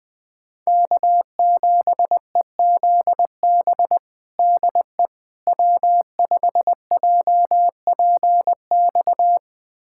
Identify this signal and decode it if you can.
Morse code, heard as a keyed tone, about 20 wpm, 700 Hz. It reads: K7EZB DE W5JPX
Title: USA Callsign DE Callsign - 5 Characters - 20wpm